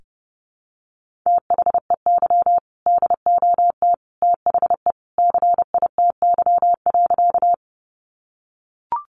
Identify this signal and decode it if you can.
Morse code, heard as a keyed tone, about 30 wpm, 700 Hz. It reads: THEY DOT THE CITY.
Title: Sentences from Top 1500 Words 30wpm